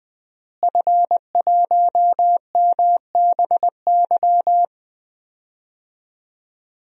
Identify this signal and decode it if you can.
Morse code, heard as a keyed tone, about 20 wpm, 700 Hz. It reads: F1MBY
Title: DXCC-100 Call Signs - 5 Characters - Repeated 3x - 20wpm